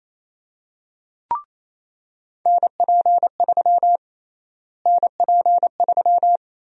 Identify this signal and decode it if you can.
Morse code, heard as a keyed tone, about 27 wpm, 700 Hz. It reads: NP3 NP3
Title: USA Call Sign Prefixes - Repeated 2x - 28wpm